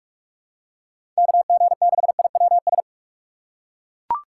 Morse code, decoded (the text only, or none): KG6IWS